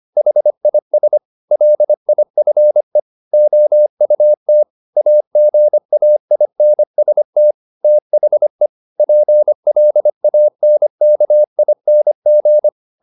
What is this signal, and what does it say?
Morse code, heard as a keyed tone, about 25 wpm, 600 Hz. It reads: HIS LIFE OUT AGAINST THE PLANKING
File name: the old man and the sea part 1 25wpm